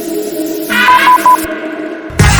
There is no code.